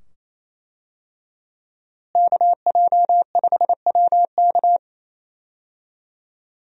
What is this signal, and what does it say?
Morse code, heard as a keyed tone, about 28 wpm, 700 Hz. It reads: KJ5WK
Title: USA Call Signs - 5 Characters - Repeated 2x - 28wpm